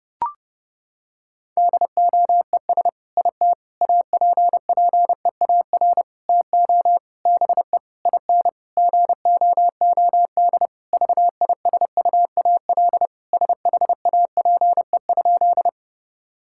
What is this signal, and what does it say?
Morse code, heard as a keyed tone, about 30 wpm, 700 Hz. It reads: DOES IT APPEAR TO BE IN GOOD VISUAL SHAPE?
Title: Sentences from Top 1000 Words - Review Part 6 - 30wpm